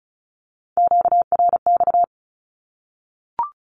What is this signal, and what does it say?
Morse code, heard as a keyed tone, about 34 wpm, 700 Hz. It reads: QRX
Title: Q-Codes Repeated 3x 35wpm